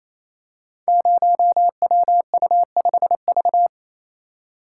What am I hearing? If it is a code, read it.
Morse code, heard as a keyed tone, about 27 wpm, 700 Hz. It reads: 0WU5V
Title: Code Groups - 5 Letter:Numbers - 28wpm